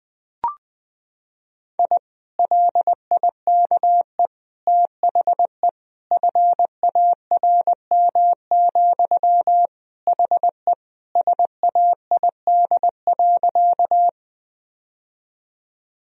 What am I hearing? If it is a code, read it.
Morse code, heard as a keyed tone, about 20 wpm, 700 Hz. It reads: I LIKE THE FARM, HE SAID.
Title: Sentences from Top 1000 Words - Review Part 2 - 20wpm